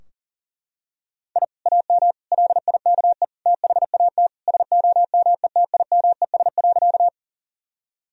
Morse code, heard as a keyed tone, about 40 wpm, 700 Hz. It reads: I AM LIKE THAT SOMETIMES.
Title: Sentences from Top 1000 Words - Review Part 3 - 40wpm